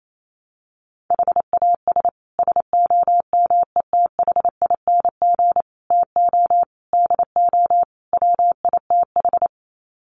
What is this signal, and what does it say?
Morse code, heard as a keyed tone, about 28 wpm, 700 Hz. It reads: HAS SOMETHING TO DO WITH
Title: Sets of 5 Words Encore 28wpm